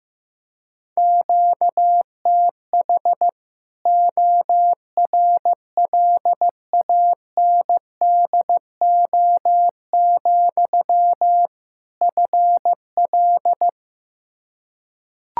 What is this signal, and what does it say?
Morse code, heard as a keyed tone, about 15 wpm, 700 Hz. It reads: QTH ORLANDO, FL
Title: Practice QSO - Exchange 1 - Beginning - 15wpm